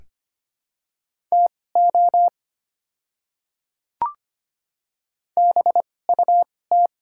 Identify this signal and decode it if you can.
Morse code, heard as a keyed tone, about 25 wpm, spaced as for 18 wpm, 700 Hz. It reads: TO BUT
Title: 2x Character Spacing - Top 200 Words - 25wpm